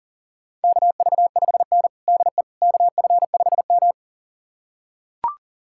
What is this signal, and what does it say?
Morse code, heard as a keyed tone, about 40 wpm, 700 Hz. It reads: KV5N DE KF5M
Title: USA Callsign DE Callsign - 4 Characters - 40wpm